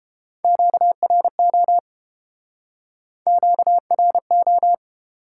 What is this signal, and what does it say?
Morse code, heard as a keyed tone, about 31 wpm, 700 Hz. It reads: QRO QRO